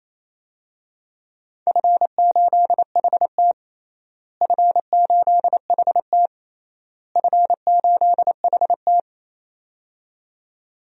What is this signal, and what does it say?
Morse code, heard as a keyed tone, about 28 wpm, 700 Hz. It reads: F8HT F8HT F8HT
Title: DXCC-100 Call Signs - 4 Characters - Repeated 3x - 28wpm